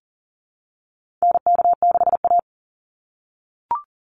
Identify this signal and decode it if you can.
Morse code, heard as a keyed tone, about 40 wpm, 700 Hz. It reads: NK6A
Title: NA SOTA Chaser Callsigns 40wpm